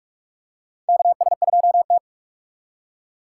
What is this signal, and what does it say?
Morse code, heard as a keyed tone, about 42 wpm, 700 Hz. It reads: KS2T
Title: USA Call Signs - 4 Characters - 4x Character Spacing - 45wpm